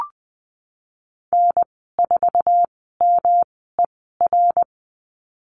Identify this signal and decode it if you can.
Morse code, heard as a keyed tone, about 20 wpm, spaced as for 15 wpm, 700 Hz. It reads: N4MER
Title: 122 Lesson 20g - Number 3 - Call Signs 20wpm